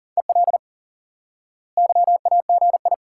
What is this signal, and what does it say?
Morse code, heard as a keyed tone, about 40 wpm, 700 Hz. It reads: EL YAGI
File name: IQE Course 3 - Lesson 89 - YAGI - Two Elements - 40wpm